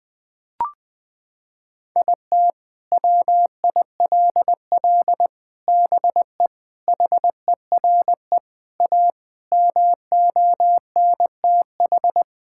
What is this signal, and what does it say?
Morse code, heard as a keyed tone, about 21 wpm, 700 Hz. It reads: IT WILL BE HERE A MONTH
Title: Sentences from Top 1000 Words - Review Part 5 - 20wpm